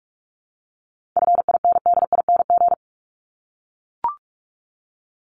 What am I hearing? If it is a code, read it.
Morse code, heard as a keyed tone, about 44 wpm, 700 Hz. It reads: FINDING
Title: Top 1001-1250 Words 45wpm